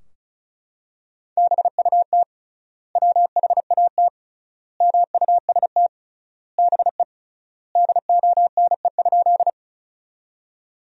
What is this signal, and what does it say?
Morse code, heard as a keyed tone, about 36 wpm, 700 Hz. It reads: BUT WHAT MUST BE DONE?
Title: 3x Word-Spacing - Sentences from Top 500 Words - 35wpm